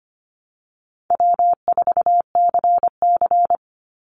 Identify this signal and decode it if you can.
Morse code, heard as a keyed tone, about 25 wpm, 700 Hz. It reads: W4CC
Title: Easy Call Signs - 25wpm